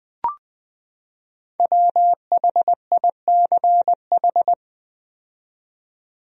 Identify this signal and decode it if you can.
Morse code, heard as a keyed tone, about 21 wpm, 700 Hz. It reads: WHICH
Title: 4x Word-Spacing - Sentences from Top 100 Words - 20wpm